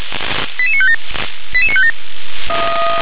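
Five-tone selective call (ZVEI), heard as quick runs of five tones, tones of about 70 ms. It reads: 88957 88957
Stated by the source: ZVEI five-tone call received on SDR.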